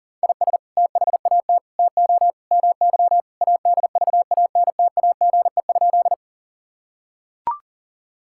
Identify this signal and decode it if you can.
Morse code, heard as a keyed tone, about 40 wpm, 700 Hz. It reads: IS THAT TO MY ADVANTAGE?
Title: Sentences from Top 1000 Words - Review Part 6 - 40wpm